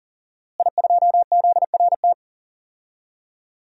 Morse code, heard as a keyed tone, about 40 wpm, 700 Hz. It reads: I2ZRT
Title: DXCC-100 Call Signs - 5 Characters - Repeated 3x - 40wpm